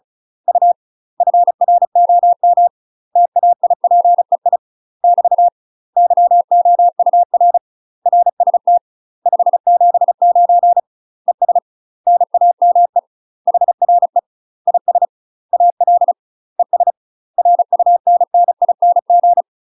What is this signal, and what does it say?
Morse code, heard as a keyed tone, about 36 wpm, 700 Hz. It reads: U FROM TAIPEI <BT> YOUR RST 579 ES NAME HRE IS AL ES RUNNING
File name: UK MASTER QSOs TRAINING/35WPM